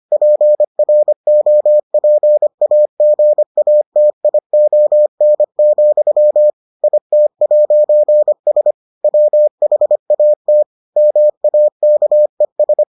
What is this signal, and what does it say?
Morse code, heard as a keyed tone, about 25 wpm, 600 Hz. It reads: PROPAGATION, IT'S WHAT MAKES